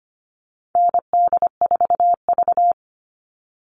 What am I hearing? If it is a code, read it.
Morse code, heard as a keyed tone, about 25 wpm, 700 Hz. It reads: ND4V
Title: Call Signs Repeated 3x - 25wpm